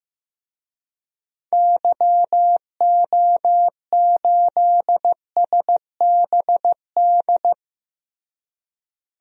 Morse code, heard as a keyed tone, about 15 wpm, 700 Hz. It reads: YO8SBD
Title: DXCC-100 Call Signs - 6 Characters - Repeated 3x - 15wpm